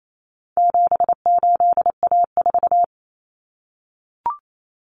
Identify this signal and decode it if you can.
Morse code, heard as a keyed tone, about 28 wpm, 700 Hz. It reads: 78A4
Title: Common Cut Numbers - 4-Digit 28wpm